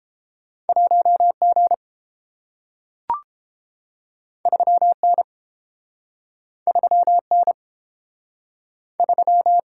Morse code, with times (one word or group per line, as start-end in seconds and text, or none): 0.7-1.3 1
1.4-1.7 G
4.5-4.9 3
5.0-5.2 N
6.7-7.2 3
7.3-7.5 N
9.0-9.6 3